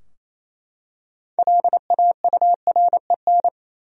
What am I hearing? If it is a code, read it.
Morse code, heard as a keyed tone, about 28 wpm, 700 Hz. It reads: LAUREN
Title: USA Common Names - 28wpm